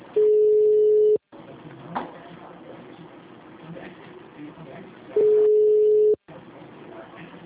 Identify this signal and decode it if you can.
Ringback tone (Europe, 425 Hz), with speech.
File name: out- -2030- - - .17304